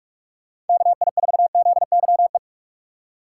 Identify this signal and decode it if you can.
Morse code, heard as a keyed tone, about 46 wpm, 700 Hz. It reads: KI4ZYE
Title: USA Call Signs - 6 Characters - 3x Character Spacing - 45wpm